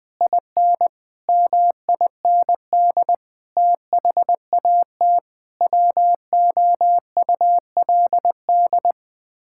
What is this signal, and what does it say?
Morse code, heard as a keyed tone, about 20 wpm, 700 Hz. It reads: IN MIND THAT WOULD
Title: Sets of 9 Words - 20wpm